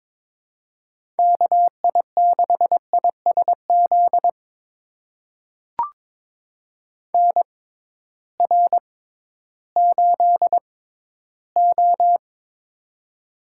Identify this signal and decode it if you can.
Morse code, heard as a keyed tone, about 22 wpm, spaced as for 13 wpm, 700 Hz. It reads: KI6ISZ NR8O
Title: USA Call Signs - 6x Character Spacing - 22wpm